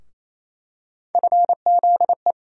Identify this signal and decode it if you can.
Morse code, heard as a keyed tone, about 28 wpm, 700 Hz. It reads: FZE
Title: USA Call Sign Postfixes - 3 Letters 28wpm